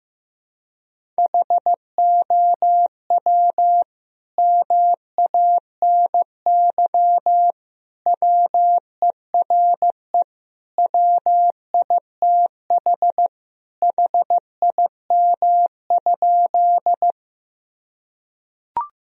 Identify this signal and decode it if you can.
Morse code, heard as a keyed tone, about 15 wpm, 700 Hz. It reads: HOW MANY WERE WITH HIM?
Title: Sentences from Top 1000 Words - Review Part 1 - 15wpm